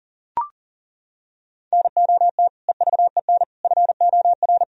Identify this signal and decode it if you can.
Morse code, heard as a keyed tone, about 40 wpm, 700 Hz. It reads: NOT EVEN FOR
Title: Sentences from Top 1000 Words - Review Part 1 - 40wpm